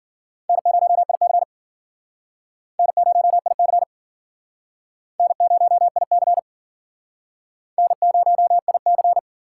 Morse code, heard as a keyed tone, about 50 wpm, 700 Hz. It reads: N0IC N0IC N0IC N0IC